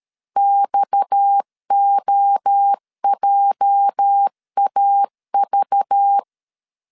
Morse code, heard as a keyed tone, about 13 wpm, 800 Hz. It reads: XOJAV
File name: LEZIONI / 10 SIGNS / 10A SLASH 12wpm 2314